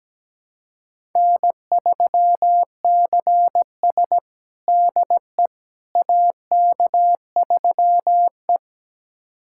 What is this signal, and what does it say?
Morse code, heard as a keyed tone, about 17 wpm, 700 Hz. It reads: N3CS DE AK3E
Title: USA Callsign DE Callsign - 4 Characters - 17wpm